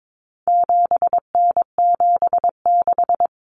Morse code, heard as a keyed tone, about 22 wpm, 700 Hz. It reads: 7N76
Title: Common Cut Numbers - 4-Digit 22wpm